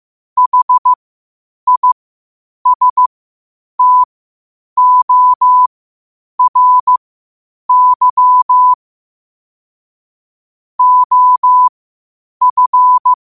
Morse code, heard as a keyed tone, about 15 wpm, spaced as for 8 wpm, 1000 Hz. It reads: HISTORY OF